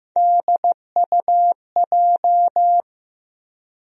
Morse code, heard as a keyed tone, about 15 wpm, 700 Hz. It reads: DUJ
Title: USA Call Sign Postfixes - 3 Letters 15wpm